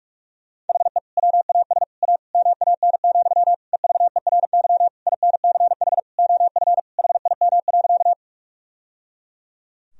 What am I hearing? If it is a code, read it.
Morse code, heard as a keyed tone, about 46 wpm, 700 Hz. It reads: HE WAS A MAN, EVERY INCH OF HIM.
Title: Sentences from Top 1000 Words - Review Part 3 - 45wpm